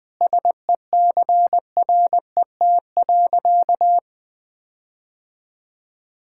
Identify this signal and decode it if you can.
Morse code, heard as a keyed tone, about 20 wpm, 700 Hz. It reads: SECRET.
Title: Sentences with 4-Word Phrases - 20wpm